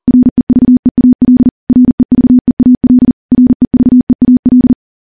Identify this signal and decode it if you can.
Morse code, heard as a keyed tone, about 43 wpm, 250 Hz. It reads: REVEAL REVEAL REVEAL
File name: morse / morse-training-40WPM